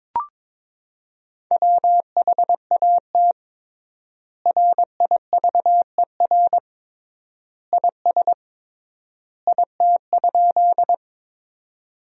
Morse code, heard as a keyed tone, about 22 wpm, 700 Hz. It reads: WHAT RIVER IS IT?
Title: 3x Word-Spacing - Sentences from Top 300 Words - 22wpm